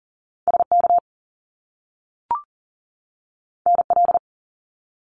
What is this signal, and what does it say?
Morse code, heard as a keyed tone, about 42 wpm, 700 Hz. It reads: SK NL
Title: Canadian Provinces and Territories 40wpm